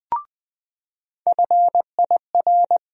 Morse code, heard as a keyed tone, about 20 wpm, 700 Hz. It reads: FIR
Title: USA Call Sign Postfixes - 3 Letters 20wpm